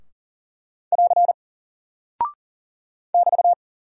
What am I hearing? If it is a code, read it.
Morse code, heard as a keyed tone, about 40 wpm, 700 Hz. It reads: <AR> <BT>
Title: Single Character (Letter:Number:Punctuation:Prosign) 40wpm